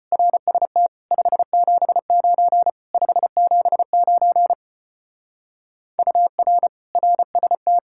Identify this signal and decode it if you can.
Morse code, heard as a keyed tone, about 32 wpm, 700 Hz. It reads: RST 579 579 UR RST